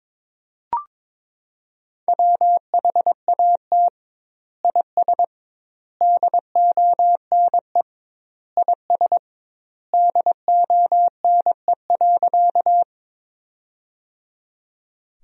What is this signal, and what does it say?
Morse code, heard as a keyed tone, about 22 wpm, 700 Hz. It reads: WHAT IS DONE IS DONE.
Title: 2x Word-Spacing - Sentences from Top 500 Words - 22wpm